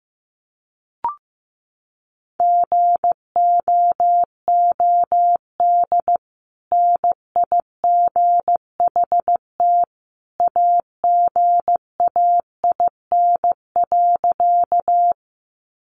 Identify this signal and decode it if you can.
Morse code, heard as a keyed tone, about 15 wpm, 700 Hz. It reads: GOOD NIGHT AGAIN.
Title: Sentences from Top 300 Words 15wpm